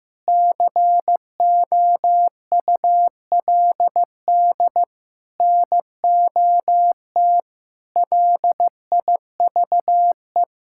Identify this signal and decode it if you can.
Morse code, heard as a keyed tone, about 15 wpm, 700 Hz. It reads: COULD NOT LIVE
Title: Sentences from Top 1600 Words 15wpm